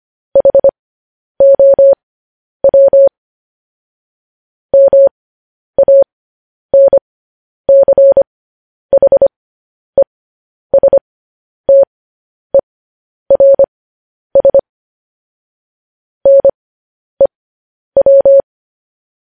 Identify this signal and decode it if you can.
Morse code, heard as a keyed tone, about 25 wpm, spaced as for 10 wpm, 550 Hz. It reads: HOW MANCHESTERS NEW